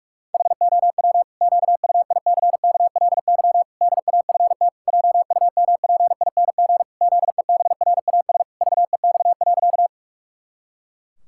Morse code, heard as a keyed tone, about 46 wpm, 700 Hz. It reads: HOW QUICKLY DAFT JUMPING ZEBRAS VEX.
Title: Pangram Challenge v2 - 45wpm